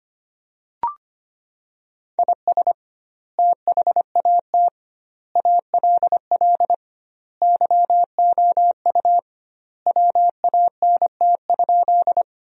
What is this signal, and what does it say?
Morse code, heard as a keyed tone, about 25 wpm, 700 Hz. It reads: IS THAT ALL YOU WANT?
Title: 2x Word-Spacing - Sentences from Top 200 Words - 25wpm